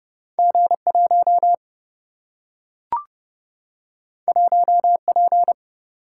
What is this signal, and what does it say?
Morse code, heard as a keyed tone, about 31 wpm, 700 Hz. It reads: G 1 1 P
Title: Single-Letter Single-Number 30wpm